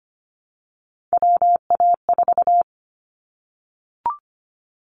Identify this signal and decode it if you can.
Morse code, heard as a keyed tone, about 25 wpm, 700 Hz. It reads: WA4
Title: Callsign Prefixes for Top 25 DXCC - Repeated 3x - 25wpm